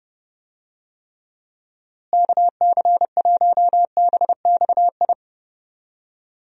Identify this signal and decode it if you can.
Morse code, heard as a keyed tone, about 30 wpm, 700 Hz. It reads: KC1BXI
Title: USA Call Signs - 6 Characters - Repeated 2x - 30wpm